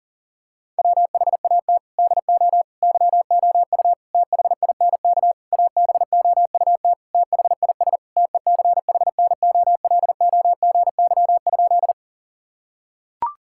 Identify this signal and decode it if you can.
Morse code, heard as a keyed tone, about 40 wpm, 700 Hz. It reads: WHAT DO YOU THINK ABOUT THIS TECHNOLOGY?
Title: Sentences from Top 1000 Words - Review Part 5 - 40wpm